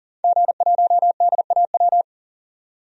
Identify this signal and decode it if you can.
Morse code, heard as a keyed tone, about 40 wpm, 700 Hz. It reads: G1DAW
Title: DXCC-100 Call Signs - 5 Characters - 40wpm